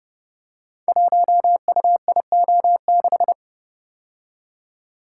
Morse code, heard as a keyed tone, about 28 wpm, 700 Hz. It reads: 1UIO6
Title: Code Groups - 5 Letter:Numbers - 30wpm